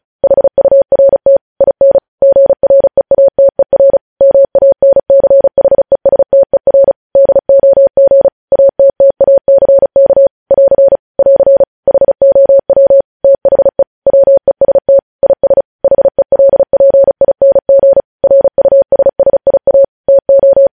Morse code, heard as a keyed tone, about 36 wpm, 550 Hz. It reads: HURT IN GREATER MANCHESTER DOG ATTACK <AR> <AR> HOW THE WEST IS HELPING RUSSIA TO
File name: podcast / News headlines-1x-35x35